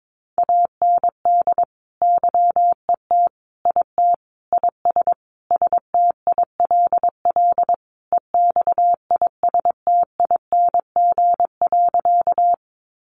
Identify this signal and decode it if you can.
Morse code, heard as a keyed tone, about 22 wpm, 700 Hz. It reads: AND YET IT IS STILL EXISTING.
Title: Sentences from Top 1500 Words 22wpm